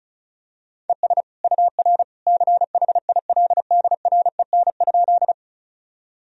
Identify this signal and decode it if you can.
Morse code, heard as a keyed tone, about 36 wpm, 700 Hz. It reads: ES UR CHILDREN?
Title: Sentences from Top 300 Words and Abbreviations 35wpm